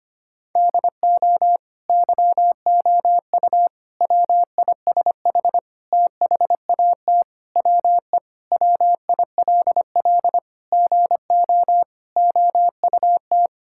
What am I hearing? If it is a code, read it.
Morse code, heard as a keyed tone, about 25 wpm, 700 Hz. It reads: DO YOU WISH THAT WE WILL GO OUT
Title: Sentences from Top 1600 Words 25wpm